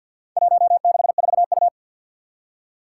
Morse code, heard as a keyed tone, about 51 wpm, 700 Hz. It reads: 164U
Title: Four Challenging Letter-Number Combinations 50wpm